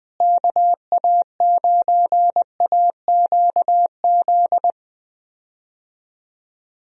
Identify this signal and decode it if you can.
Morse code, heard as a keyed tone, about 20 wpm, 700 Hz. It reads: KA9AQZ
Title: USA Call Signs - 6 Characters - Repeated 3x - 20wpm